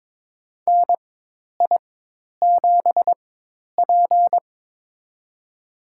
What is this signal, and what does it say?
Morse code, heard as a keyed tone, about 22 wpm, spaced as for 10 wpm, 700 Hz. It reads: NI7P
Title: USA Call Signs - 4 Characters - 4x Character Spacing - 22wpm